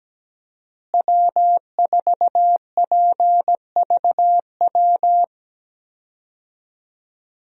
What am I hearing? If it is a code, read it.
Morse code, heard as a keyed tone, about 17 wpm, 700 Hz. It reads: W4PVW
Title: USA Call Signs - 5 Characters - Repeated 3x - 17wpm